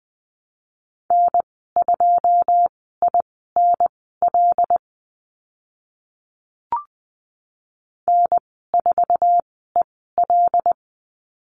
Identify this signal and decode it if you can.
Morse code, heard as a keyed tone, about 20 wpm, spaced as for 14 wpm, 700 Hz. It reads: N2INL N4EL
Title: 082 Lesson 14g - Number 2 - Call Signs 20wpm